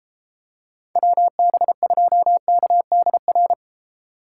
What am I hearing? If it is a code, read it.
Morse code, heard as a keyed tone, about 29 wpm, 700 Hz. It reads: WB2KDR